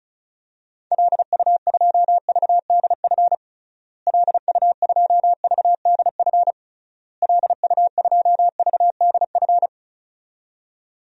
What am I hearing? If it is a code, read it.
Morse code, heard as a keyed tone, about 36 wpm, 700 Hz. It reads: LU2VDF LU2VDF LU2VDF
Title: DXCC-100 Call Signs - 6 Characters - Repeated 3x - 35wpm